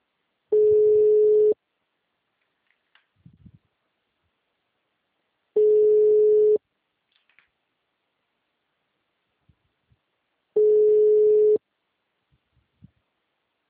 Ringback tone (Europe, 425 Hz).
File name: out- -2011- - - .27